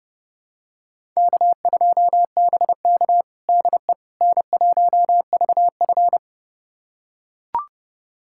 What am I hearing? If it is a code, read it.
Morse code, heard as a keyed tone, about 30 wpm, 700 Hz. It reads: K2BK DE N1VF
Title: USA Callsign DE Callsign - 4 Characters - 30wpm